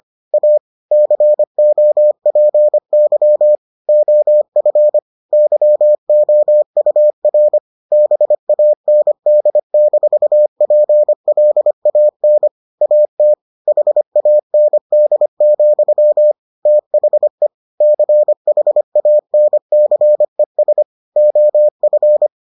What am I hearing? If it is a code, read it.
Morse code, heard as a keyed tone, about 25 wpm, 600 Hz. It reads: A COPY OF YOUR BAND-PLAN AT HAND, THE CHANCES OF